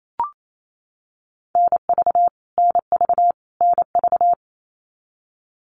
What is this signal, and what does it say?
Morse code, heard as a keyed tone, about 28 wpm, 700 Hz. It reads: NV NV NV
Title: US States Repeated 3x 28wpm